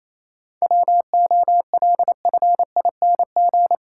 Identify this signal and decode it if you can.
Morse code, heard as a keyed tone, about 27 wpm, 700 Hz. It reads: WOLFING
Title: Pangram Challenge v2 - 28wpm